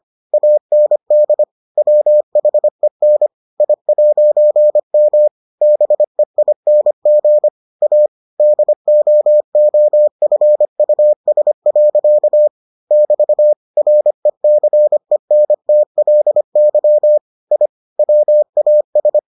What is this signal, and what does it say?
Morse code, heard as a keyed tone, about 25 wpm, 600 Hz. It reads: AND WHEN I'M BEING A DOOFUS. <BT> RECENTLY I WAS